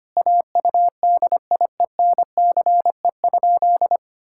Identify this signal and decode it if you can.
Morse code, heard as a keyed tone, about 25 wpm, 700 Hz. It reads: AUDIENCE?
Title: Sentences from Top 1000 Words - Review Part 5 - 25wpm